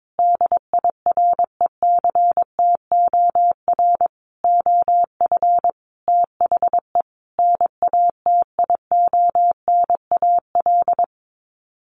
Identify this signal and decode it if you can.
Morse code, heard as a keyed tone, about 22 wpm, 700 Hz. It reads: DIRECTOR OF THE NATIONAL
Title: Sets of 5 Words Encore2 22wpm